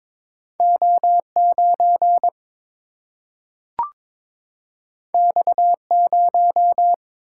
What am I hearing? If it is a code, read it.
Morse code, heard as a keyed tone, about 22 wpm, 700 Hz. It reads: O 9 X 0
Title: Single-Letter Single-Number 22wpm